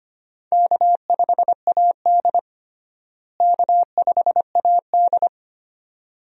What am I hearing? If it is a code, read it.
Morse code, heard as a keyed tone, about 25 wpm, 700 Hz. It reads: K5AD K5AD
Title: USA Call Signs - 4 Characters - Repeated 2x - 25wpm